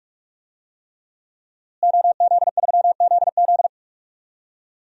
Morse code, heard as a keyed tone, about 43 wpm, 700 Hz. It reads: OZ3ZZ
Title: DXCC-100 Call Signs - 5 Characters - Repeated 3x - 45wpm